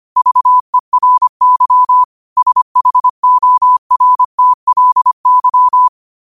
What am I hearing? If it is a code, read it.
Morse code, heard as a keyed tone, about 25 wpm, 1000 Hz. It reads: VERY SHORTLY
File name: qotdfiles / 25wpm